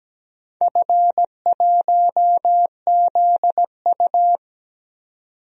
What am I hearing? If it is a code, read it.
Morse code, heard as a keyed tone, about 17 wpm, 700 Hz. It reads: F1ZU
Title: DXCC-100 Call Signs - 4 Characters - 17wpm